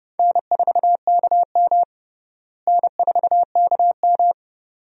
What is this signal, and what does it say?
Morse code, heard as a keyed tone, about 32 wpm, 700 Hz. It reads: N4KM N4KM
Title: USA Call Signs - 4 Characters - Repeated 2x - 30wpm